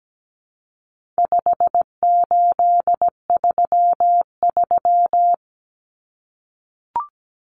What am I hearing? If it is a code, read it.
Morse code, heard as a keyed tone, about 17 wpm, 700 Hz. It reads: 5833
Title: Numbers - 4-Digit 17wpm